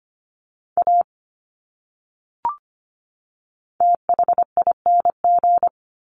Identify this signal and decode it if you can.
Morse code, heard as a keyed tone, about 25 wpm, 700 Hz. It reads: A THING
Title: Top 200 Words 25wpm